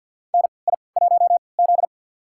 Morse code, heard as a keyed tone, about 51 wpm, spaced as for 28 wpm, 700 Hz. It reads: NI1Z
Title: USA Call Signs - 4 Characters - 3x Character Spacing - 50wpm